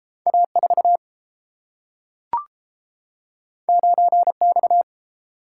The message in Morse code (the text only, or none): A4 9X